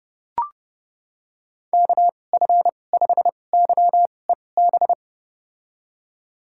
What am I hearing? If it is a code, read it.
Morse code, heard as a keyed tone, about 30 wpm, spaced as for 22 wpm, 700 Hz. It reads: KF5YEB
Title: USA Call Signs - 6 Characters - 2x Character Spacing - 30wpm